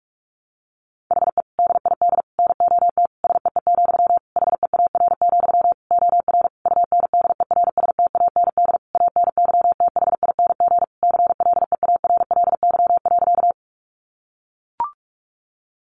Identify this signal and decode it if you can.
Morse code, heard as a keyed tone, about 44 wpm, 700 Hz. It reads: HE DID NOT SEE, HEAR, OR UNDERSTAND ANYTHING CLEARLY.
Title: Sentences from Top 1300 Words 45wpm